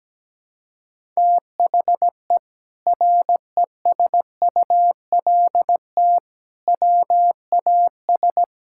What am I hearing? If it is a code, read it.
Morse code, heard as a keyed tone, about 17 wpm, 700 Hz. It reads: THE RESULT WAS
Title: Sentences from Top 1000 Words - Review Part 6 - 17wpm